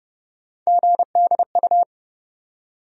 Morse code, heard as a keyed tone, about 30 wpm, 700 Hz. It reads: GDU
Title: USA Call Sign Postfixes - 3 Letters 30wpm